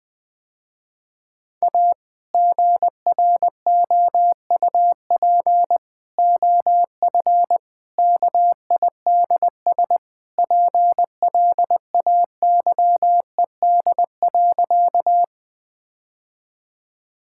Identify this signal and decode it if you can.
Morse code, heard as a keyed tone, about 20 wpm, 700 Hz. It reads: A GROUP OF KIDS PLAYED.
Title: Sentences with 3-Word Phrases - Spoken Beginning - 20wpm